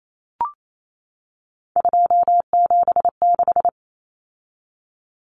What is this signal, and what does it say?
Morse code, heard as a keyed tone, about 28 wpm, 700 Hz. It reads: 276
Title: Common Cut Numbers - 3-Digit 28wpm